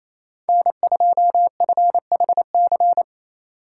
Morse code, heard as a keyed tone, about 28 wpm, 700 Hz. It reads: N2FHC
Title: USA Call Signs - 4x Character Spacing - 28wpm